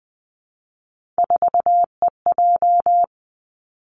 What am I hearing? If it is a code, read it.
Morse code, heard as a keyed tone, about 20 wpm, 700 Hz. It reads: 4EJ
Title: Three Challenging Letter-Number Combinations 20wpm